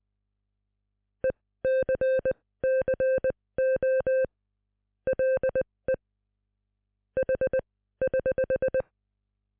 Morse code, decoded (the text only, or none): ECCO LE H <HH>